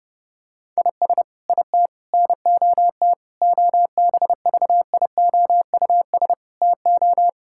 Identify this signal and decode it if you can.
Morse code, heard as a keyed tone, about 30 wpm, 700 Hz. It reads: IS IT NOT OBVIOUS TO
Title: Sentences from Top 1600 Words 30wpm